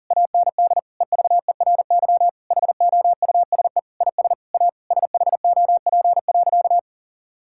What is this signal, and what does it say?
Morse code, heard as a keyed tone, about 40 wpm, 700 Hz. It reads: AND EVERY HOUSE IS A SHOP.
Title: Sentences from Top 1600 Words 40wpm